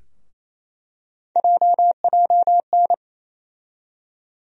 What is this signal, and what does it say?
Morse code, heard as a keyed tone, about 28 wpm, 700 Hz. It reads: JJN
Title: USA Call Sign Postfixes - 3 Letters 28wpm